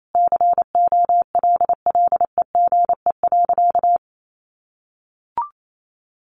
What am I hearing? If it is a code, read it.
Morse code, heard as a keyed tone, about 28 wpm, 700 Hz. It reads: COLLEGE.
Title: Sentences from Top 1300 Words 28wpm